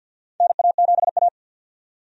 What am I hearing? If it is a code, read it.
Morse code, heard as a keyed tone, about 53 wpm, 700 Hz. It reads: NA7A
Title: USA Call Signs - 4 Characters - 50wpm